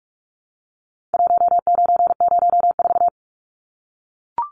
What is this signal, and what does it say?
Morse code, heard as a keyed tone, about 44 wpm, 700 Hz. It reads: 1904
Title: Numbers - 4-Digit 45wpm